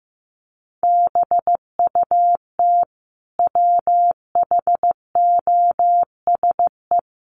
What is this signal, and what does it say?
Morse code, heard as a keyed tone, about 15 wpm, 700 Hz. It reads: BUT WHOSE
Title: Sentences from Top 1500 Words 15wpm